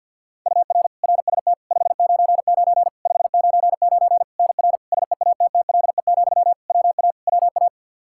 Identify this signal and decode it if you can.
Morse code, heard as a keyed tone, about 52 wpm, 700 Hz. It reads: UR RST 599 599 NR SEATTLE, WA WA
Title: Practice QSO - Exchange 1 - Beginning - 50wpm